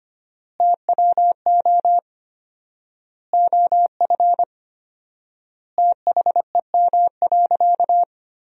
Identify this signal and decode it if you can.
Morse code, heard as a keyed tone, about 25 wpm, 700 Hz. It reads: TWO OF THEM.
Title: 4x Word-Spacing - Sentences from Top 100 Words - 25wpm